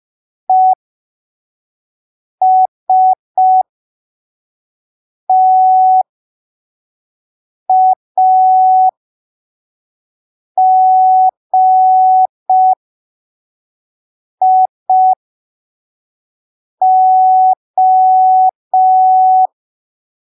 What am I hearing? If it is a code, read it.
Morse code, heard as a keyed tone, about 5 wpm, spaced as for 3 wpm, 750 Hz. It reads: ESTAGIO
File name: arquivos / texto / texto 05 5ppm